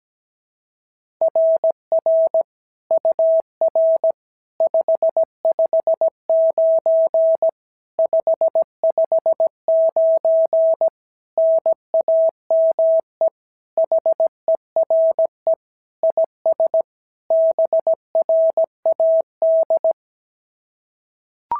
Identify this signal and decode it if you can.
Morse code, heard as a keyed tone, about 17 wpm, 650 Hz. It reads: RR UR 559 559 NAME HERE IS BRAD
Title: SOTA QSO Practice 17wpm